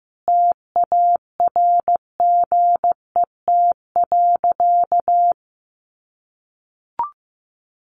Morse code, heard as a keyed tone, about 15 wpm, 700 Hz. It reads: TARGET.
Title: Sentences from Top 1200 Words 15wpm